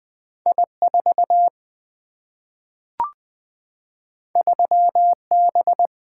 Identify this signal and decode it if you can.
Morse code, heard as a keyed tone, about 20 wpm, 700 Hz. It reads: I 4 3 B
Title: Single-Letter Single-Number 20wpm